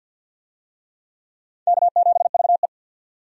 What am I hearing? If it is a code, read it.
Morse code, heard as a keyed tone, about 52 wpm, 700 Hz. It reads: K7VE
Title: USA Call Signs - 4 Characters - 50wpm